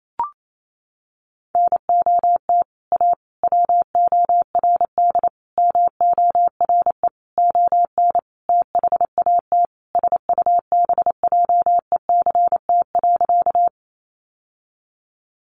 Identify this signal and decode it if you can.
Morse code, heard as a keyed tone, about 28 wpm, 700 Hz. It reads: NOT A WORD MORE ON THAT SUBJECT.
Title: Sentences from Top 800 Words 28wpm